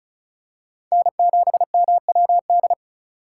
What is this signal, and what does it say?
Morse code, heard as a keyed tone, about 36 wpm, 700 Hz. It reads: N7MWD
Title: USA Call Signs - 5 Characters - 3x Character Spacing - 35wpm